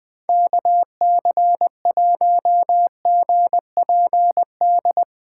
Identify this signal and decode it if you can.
Morse code, heard as a keyed tone, about 22 wpm, 700 Hz. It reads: KC1GPD
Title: Speed Racing - USA 2x3 Call Signs 20wpm